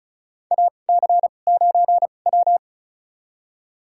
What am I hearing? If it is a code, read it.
Morse code, heard as a keyed tone, about 36 wpm, spaced as for 25 wpm, 700 Hz. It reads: AC9W
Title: USA Call Signs - 4 Characters - 2x Character Spacing - 35wpm